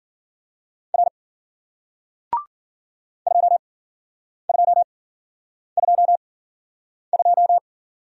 Morse code, heard as a keyed tone, about 53 wpm, spaced as for 9 wpm, 700 Hz. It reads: R 2222